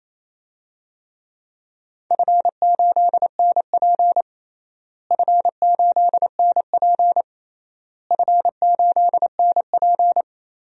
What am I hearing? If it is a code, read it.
Morse code, heard as a keyed tone, about 27 wpm, 700 Hz. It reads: F8NP F8NP F8NP
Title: DXCC-100 Call Signs - 4 Characters - Repeated 3x - 28wpm